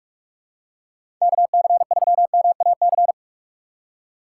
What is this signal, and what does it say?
Morse code, heard as a keyed tone, about 46 wpm, 700 Hz. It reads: KC3MAC
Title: USA Call Signs - 6 Characters - 4x Character Spacing - 45wpm